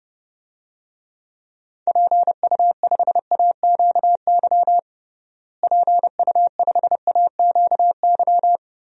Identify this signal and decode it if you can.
Morse code, heard as a keyed tone, about 31 wpm, 700 Hz. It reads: PU5AQY PU5AQY
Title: DXCC-100 Call Signs - 6 Characters - Repeated 2x - 30wpm